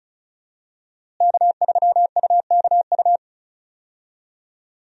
Morse code, heard as a keyed tone, about 34 wpm, 700 Hz. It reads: K3UKU
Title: USA Call Signs - 5 Characters - Repeated 3x - 35wpm